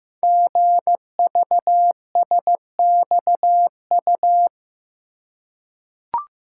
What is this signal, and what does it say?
Morse code, heard as a keyed tone, about 15 wpm, 700 Hz. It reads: GVSXU
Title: Code Groups - 5 Letters - 15wpm